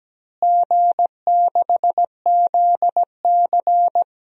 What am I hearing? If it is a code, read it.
Morse code, heard as a keyed tone, about 17 wpm, 700 Hz. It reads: G6ZC
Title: DXCC-100 Call Signs - 4 Characters - Repeated 2x - 17wpm